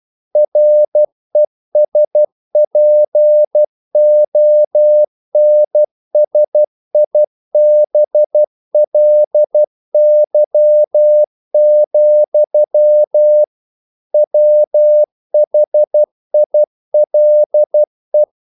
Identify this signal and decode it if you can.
Morse code, heard as a keyed tone, about 12 wpm, 600 Hz. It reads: RESPONSIBLY, WHILE
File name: CW / Practice / EnglishQuotations0112